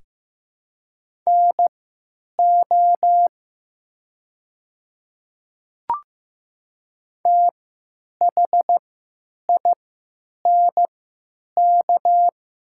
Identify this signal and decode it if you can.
Morse code, heard as a keyed tone, about 15 wpm, spaced as for 8 wpm, 700 Hz. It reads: NO THINK
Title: 3x Character Spacing - Top 200 Words - 15wpm